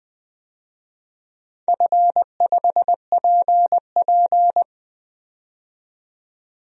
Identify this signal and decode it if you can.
Morse code, heard as a keyed tone, about 20 wpm, 700 Hz. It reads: F5PP
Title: DXCC-100 Call Signs - 4 Characters - 20wpm